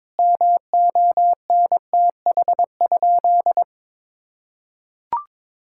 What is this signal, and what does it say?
Morse code, heard as a keyed tone, about 22 wpm, 700 Hz. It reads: MONTH?
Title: Sentences from Top 1000 Words - Review Part 6 - 22wpm